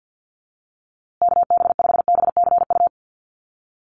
Morse code, heard as a keyed tone, about 50 wpm, 700 Hz. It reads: KB5BCU
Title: USA Call Signs Repeated 2x 50wpm